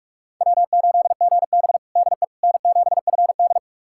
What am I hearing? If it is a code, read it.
Morse code, heard as a keyed tone, about 42 wpm, 700 Hz. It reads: W8GB DE N7FD
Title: USA Callsign DE Callsign - 4 Characters - 45wpm